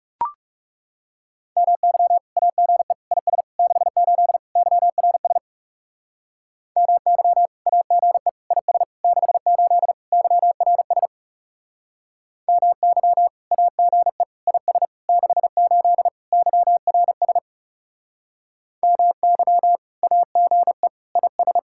Morse code, heard as a keyed tone, about 38 wpm, 700 Hz. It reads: MY AGE IS 68 YRS MY AGE IS 68 YRS MY AGE IS 68 YRS MY AGE IS